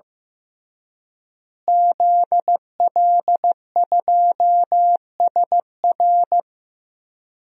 Morse code, heard as a keyed tone, about 15 wpm, 700 Hz. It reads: ZL2SR
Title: DXCC-100 Call Signs - 5 Characters - Repeated 3x - 15wpm